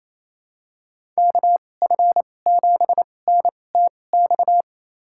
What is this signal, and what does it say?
Morse code, heard as a keyed tone, about 28 wpm, spaced as for 20 wpm, 700 Hz. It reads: KF7NTX
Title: USA Call Signs - 6 Characters - 2x Character Spacing - 28wpm